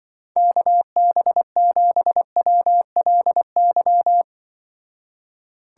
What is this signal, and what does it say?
Morse code, heard as a keyed tone, about 25 wpm, 700 Hz. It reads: KB7WLY
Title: Speed Racing - USA 2x3 Call Signs 20wpm